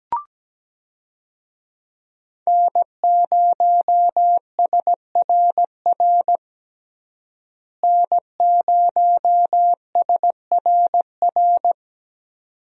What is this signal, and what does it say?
Morse code, heard as a keyed tone, about 17 wpm, 700 Hz. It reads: N0SRR N0SRR
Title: USA Call Signs - 5 Characters - Repeated 2x - 17wpm